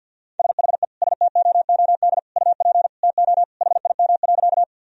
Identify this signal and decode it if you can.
Morse code, heard as a keyed tone, about 49 wpm, 700 Hz. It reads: SHE STOOD UP TO HIM.
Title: Sentences from Top 1000 Words - Review Part 2 - 50wpm